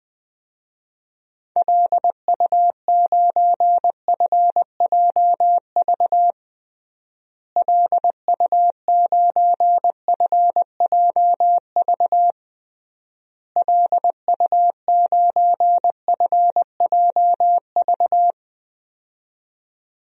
Morse code, heard as a keyed tone, about 20 wpm, 700 Hz. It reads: LU9FJV LU9FJV LU9FJV
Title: DXCC-100 Call Signs - 6 Characters - Repeated 3x - 20wpm